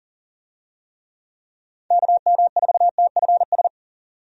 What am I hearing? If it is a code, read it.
Morse code, heard as a keyed tone, about 40 wpm, 700 Hz. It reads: KM4TFS
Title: USA Call Signs - 6 Characters - Repeated 3x - 40wpm